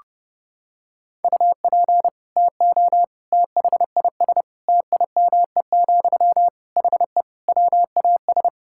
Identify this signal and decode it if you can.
Morse code, heard as a keyed tone, about 30 wpm, 700 Hz. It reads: UP TO THIS TIME, HE WAS
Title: Sentences from Top 1000 Words - Review Part 6 - 30wpm